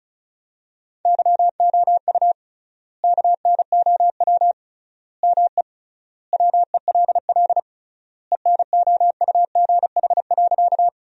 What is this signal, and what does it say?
Morse code, heard as a keyed tone, about 36 wpm, 700 Hz. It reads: YOU KNOW ME WELL ENOUGH.
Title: 3x Word-Spacing - Sentences from Top 400 Words - 35wpm